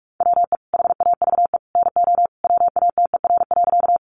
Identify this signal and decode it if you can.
Morse code, heard as a keyed tone, about 44 wpm, 700 Hz. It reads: WE HAVE NO WATER.
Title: Sentences from Top 100 Words 45wpm